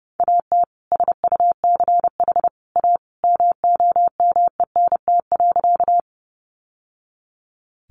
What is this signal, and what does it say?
Morse code, heard as a keyed tone, about 30 wpm, 700 Hz. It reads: AT SUCH A MOMENT.
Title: Sentences from Top 1000 Words 30wpm